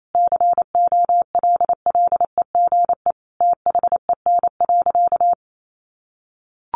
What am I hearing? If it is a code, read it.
Morse code, heard as a keyed tone, about 28 wpm, 700 Hz. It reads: COLLEGE THEN.
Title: Sentences from Top 1300 Words 28wpm